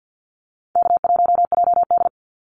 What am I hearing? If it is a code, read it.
Morse code, heard as a keyed tone, about 50 wpm, 700 Hz. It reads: K1JD
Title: NA SOTA Activator Callsigns 50wpm